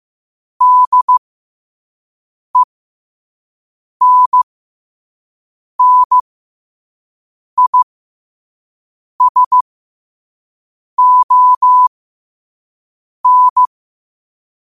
Morse code, heard as a keyed tone, about 15 wpm, spaced as for 5 wpm, 1000 Hz. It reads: DENNISON
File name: qotdfiles / 5wpm